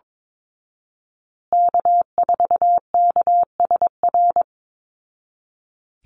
Morse code, heard as a keyed tone, about 22 wpm, 700 Hz. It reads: K4KSR
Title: Call Signs Repeated 2x - 22wpm